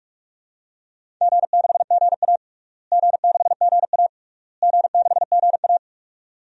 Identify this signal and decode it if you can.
Morse code, heard as a keyed tone, about 46 wpm, 700 Hz. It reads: G6GA G6GA G6GA
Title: DXCC-100 Call Signs - 4 Characters - Repeated 3x - 45wpm